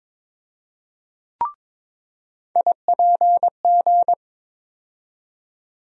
Morse code, heard as a keyed tone, about 22 wpm, 700 Hz. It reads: IPG
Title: USA Call Sign Postfixes - 3 Letters 22wpm